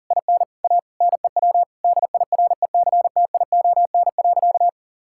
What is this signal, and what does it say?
Morse code, heard as a keyed tone, about 40 wpm, 700 Hz. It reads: IN A NEW DIRECTION.
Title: Sentences from Top 1000 Words - Review Part 5 - 40wpm